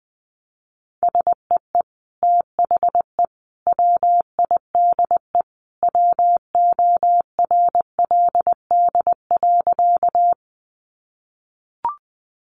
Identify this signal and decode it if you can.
Morse code, heard as a keyed tone, about 20 wpm, 700 Hz. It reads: SEE THE WIDE WORLD.
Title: Sentences from Top 900 Words 20wpm